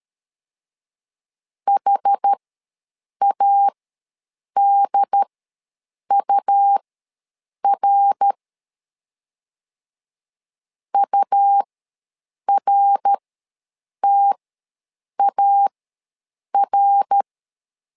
Morse code, heard as a keyed tone, about 13 wpm, spaced as for 7 wpm, 800 Hz. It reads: HADUR URTAR
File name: LEZIONI / 3 MWRO / 3 MWR ws 12wpm 1809